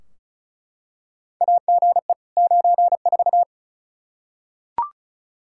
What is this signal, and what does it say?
Morse code, heard as a keyed tone, about 36 wpm, 700 Hz. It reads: AGE 94
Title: Practice QSO - Exchange 3 - More Info - 35wpm